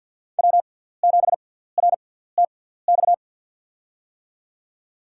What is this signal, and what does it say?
Morse code, heard as a keyed tone, about 46 wpm, spaced as for 17 wpm, 700 Hz. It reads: W7RTX
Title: USA Call Signs - 6x Character Spacing - 50wpm